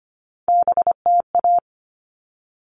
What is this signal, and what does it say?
Morse code, heard as a keyed tone, about 25 wpm, 700 Hz. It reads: BTA
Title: All Cut Numbers - 3-Digit 25wpm